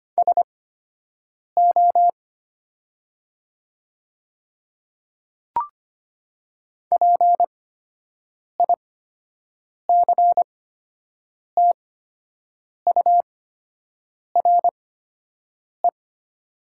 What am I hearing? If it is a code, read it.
Morse code, heard as a keyed tone, about 25 wpm, spaced as for 7 wpm, 700 Hz. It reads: SO PICTURE
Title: 8x Character Spacing - Top 500 Words - 25wpm